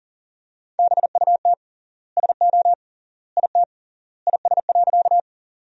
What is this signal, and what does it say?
Morse code, heard as a keyed tone, about 40 wpm, 700 Hz. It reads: BUT SO IT IS.
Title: 3x Word-Spacing - Sentences from Top 100 Words - 40wpm